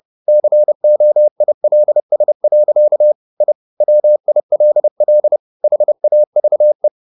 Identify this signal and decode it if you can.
Morse code, heard as a keyed tone, about 30 wpm, 600 Hz. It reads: COILS. I WILL HAVE
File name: the old man and the sea / the old man and the sea 1 30wpm